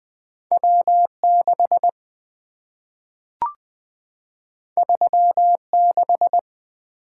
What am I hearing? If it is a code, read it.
Morse code, heard as a keyed tone, about 20 wpm, 700 Hz. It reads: W 6 3 6
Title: Two Letter-Number Combinations - 20wpm